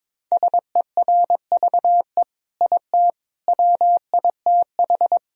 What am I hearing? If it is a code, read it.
Morse code, heard as a keyed tone, about 22 wpm, 700 Hz. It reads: SERVE IT WITH
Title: Sentences from Top 1000 Words - Review Part 3 - 22wpm